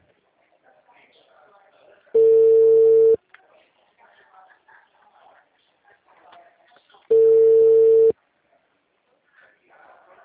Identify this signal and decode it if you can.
Ringback tone (China, 450 Hz), with speech.